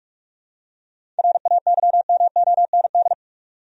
Morse code, heard as a keyed tone, about 43 wpm, 700 Hz. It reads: RAYMOND
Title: USA Common Names - 45wpm